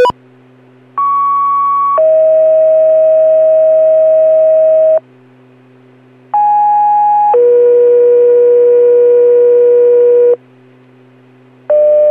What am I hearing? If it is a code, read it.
Two-tone page, heard as a one-second tone, then a three-second tone, about 1120 then 640 Hz, about 830 then 480 Hz. It is heard from 1.0 to 10.4 s.